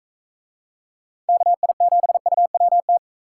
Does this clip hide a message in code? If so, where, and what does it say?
Yes: Morse code from 1.3 to 3.0 s, heard as a keyed tone, about 41 wpm, 700 Hz. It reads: KI7UWT